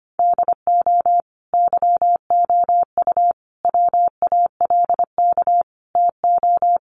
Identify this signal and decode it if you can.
Morse code, heard as a keyed tone, about 25 wpm, 700 Hz. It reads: DO YOU WALK TO
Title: Sentences from Top 1400 Words 25wpm